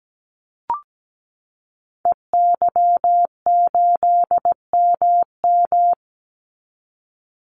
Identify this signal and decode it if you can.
Morse code, heard as a keyed tone, about 17 wpm, 700 Hz. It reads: EY8MM
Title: Call Signs Repeated 3x - 17wpm